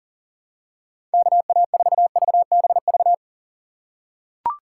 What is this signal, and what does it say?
Morse code, heard as a keyed tone, about 40 wpm, 700 Hz. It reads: KA4VBV
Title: USA Call Signs - 6 Characters - 2x Character Spacing - 40wpm